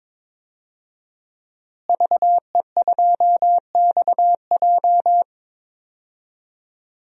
Morse code, heard as a keyed tone, about 22 wpm, 700 Hz. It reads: VE2XJ
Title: DXCC-100 Call Signs - 5 Characters - Repeated 3x - 22wpm